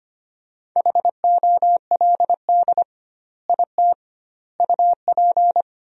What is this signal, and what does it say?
Morse code, heard as a keyed tone, about 25 wpm, 700 Hz. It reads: HOLD IT UP
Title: 2x Word-Spacing - Sentences from Top 400 Words - 25wpm